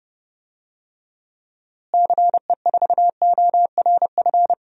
Morse code, heard as a keyed tone, about 30 wpm, 700 Hz. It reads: CE4ORF
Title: DXCC-100 Call Signs - 6 Characters - Repeated 3x - 30wpm